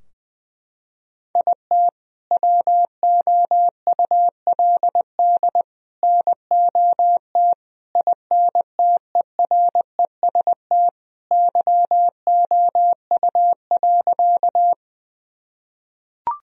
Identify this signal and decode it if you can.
Morse code, heard as a keyed tone, about 21 wpm, 700 Hz. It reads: IT WOULD NOT INTEREST YOU.
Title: Sentences from Top 1000 Words - Review Part 2 - 20wpm